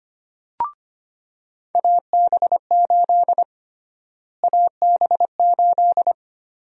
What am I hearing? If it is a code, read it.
Morse code, heard as a keyed tone, about 25 wpm, 700 Hz. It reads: AB8 AB8
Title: USA Call Sign Prefixes - Repeated 2x - 25wpm